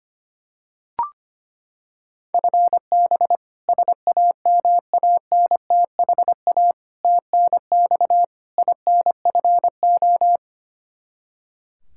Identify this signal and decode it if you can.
Morse code, heard as a keyed tone, about 25 wpm, 700 Hz. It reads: FB SAMANTHA TNX INFO
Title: K1USN SST QSO Practice 25wpm